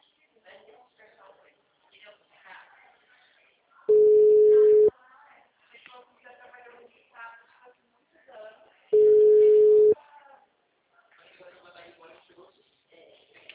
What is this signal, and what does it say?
Ringback tone (Europe, 425 Hz), with speech.
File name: out- -2006- - - .21828